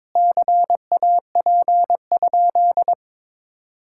Morse code, heard as a keyed tone, about 21 wpm, 700 Hz. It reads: CAP?
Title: Sentences from Top 1600 Words 22wpm